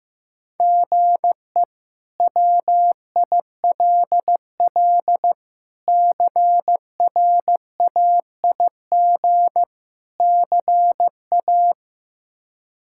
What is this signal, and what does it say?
Morse code, heard as a keyed tone, about 15 wpm, 700 Hz. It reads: GE WILL CRAIG CA
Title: K1USN SST QSO Practice - 15wpm